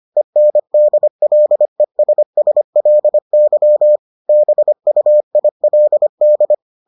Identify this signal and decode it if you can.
Morse code, heard as a keyed tone, about 25 wpm, 600 Hz. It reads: ENDLESSLY BUILD